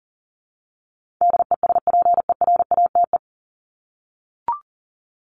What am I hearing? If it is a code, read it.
Morse code, heard as a keyed tone, about 40 wpm, 700 Hz. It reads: DESPERATE
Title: Top 1751-2000 Words 40wpm